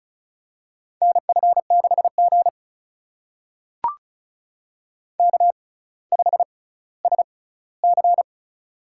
Morse code, heard as a keyed tone, about 36 wpm, spaced as for 18 wpm, 700 Hz. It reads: NF6G K5SC
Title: USA Call Signs - 4 Characters - 6x Character Spacing - 35wpm